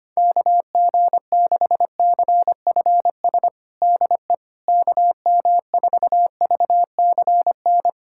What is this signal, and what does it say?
Morse code, heard as a keyed tone, about 25 wpm, 700 Hz. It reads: KG6CFS DE KM4VCN
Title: USA Callsign DE Callsign - 6 Characters - 25wpm